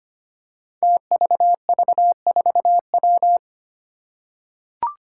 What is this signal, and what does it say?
Morse code, heard as a keyed tone, about 25 wpm, 700 Hz. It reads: TVV4W
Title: Five Challenging Letter-Number Combinations 25wpm